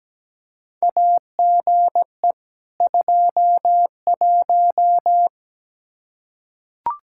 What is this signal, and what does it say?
Morse code, heard as a keyed tone, about 17 wpm, 700 Hz. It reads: AGE 21
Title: Practice QSO - Exchange 3 - More Info - 17wpm